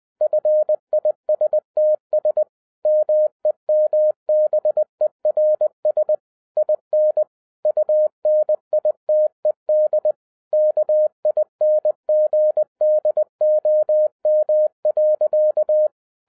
Morse code, heard as a keyed tone, about 20 wpm, 610 Hz. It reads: FISTS MEMBERS IN UNITED KINGDOM.